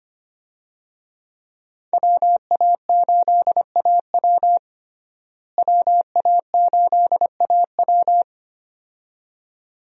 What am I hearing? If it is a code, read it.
Morse code, heard as a keyed tone, about 25 wpm, 700 Hz. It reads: WA8AW WA8AW
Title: USA Call Signs - 5 Characters - Repeated 2x - 25wpm